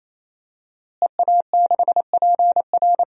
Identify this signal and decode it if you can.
Morse code, heard as a keyed tone, about 28 wpm, 700 Hz. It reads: EA6PR
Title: DXCC-100 Call Signs - 5 Characters - Repeated 3x - 28wpm